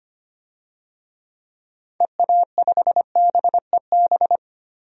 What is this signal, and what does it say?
Morse code, heard as a keyed tone, about 25 wpm, 700 Hz. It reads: EA5BEB
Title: DXCC-100 Call Signs - 6 Characters - 25wpm